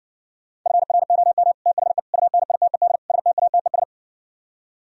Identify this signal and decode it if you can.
Morse code, heard as a keyed tone, about 68 wpm, 700 Hz. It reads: FROM THE UNITED STATES